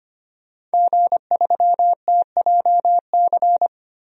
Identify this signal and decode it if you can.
Morse code, heard as a keyed tone, about 25 wpm, 700 Hz. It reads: G3TJC
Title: DXCC-100 Call Signs - 5 Characters - 25wpm